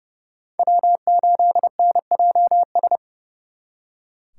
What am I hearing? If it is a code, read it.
Morse code, heard as a keyed tone, about 38 wpm, 700 Hz. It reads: W8NJS